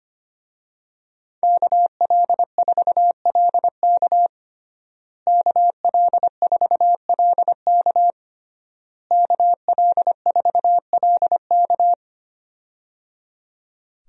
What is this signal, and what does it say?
Morse code, heard as a keyed tone, about 25 wpm, 700 Hz. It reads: KL4LK KL4LK KL4LK
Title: USA Call Signs - 5 Characters - Repeated 3x - 25wpm